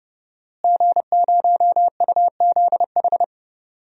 Morse code, heard as a keyed tone, about 30 wpm, 700 Hz. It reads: G0UZH
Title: DXCC-100 Call Signs - 5 Characters - Repeated 3x - 30wpm